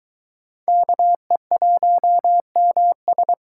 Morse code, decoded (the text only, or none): KE1MS